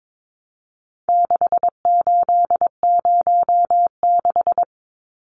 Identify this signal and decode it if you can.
Morse code, heard as a keyed tone, about 22 wpm, 700 Hz. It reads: 6806
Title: Numbers - 4-Digit 22wpm